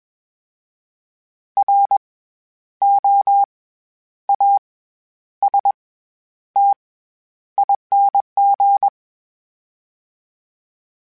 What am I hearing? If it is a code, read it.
Morse code, heard as a keyed tone, about 21 wpm, spaced as for 10 wpm, 800 Hz. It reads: ROASTING
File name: Special INGs-mp3s / Special INGs-mp3s / SlowToFaster ing @20